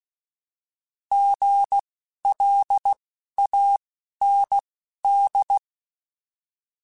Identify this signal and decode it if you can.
Morse code, heard as a keyed tone, about 16 wpm, spaced as for 11 wpm, 800 Hz. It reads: GLAND
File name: Alphabetical / Alphabetical mp3s / Gs @10